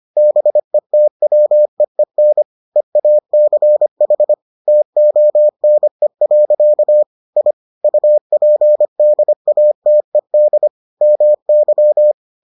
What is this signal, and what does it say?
Morse code, heard as a keyed tone, about 25 wpm, 600 Hz. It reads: BETWEEN EACH TONE. I UPDATED MY